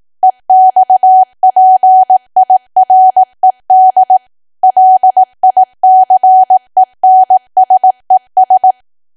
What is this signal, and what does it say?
Morse code, heard as a keyed tone, about 18 wpm, 750 Hz. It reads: EXPIRED LICENSES
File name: cwa / W1AW-Level2 / 18WPM